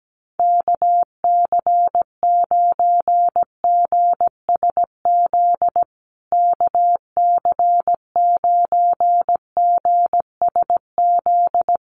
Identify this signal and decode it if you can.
Morse code, heard as a keyed tone, about 17 wpm, 700 Hz. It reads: KC9GSZ KC9GSZ
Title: USA Call Signs Repeated 3x 17wpm